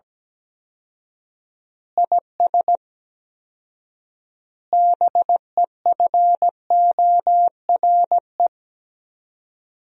Morse code, heard as a keyed tone, about 17 wpm, 700 Hz. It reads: IS BEFORE
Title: 4x Word-Spacing - Sentences from Top 200 Words - 17wpm